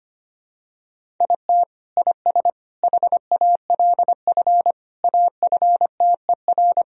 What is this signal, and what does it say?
Morse code, heard as a keyed tone, about 25 wpm, 700 Hz. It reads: IT IS HALF AFTER
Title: Sentences from Top 1000 Words - Review Part 1 - 25wpm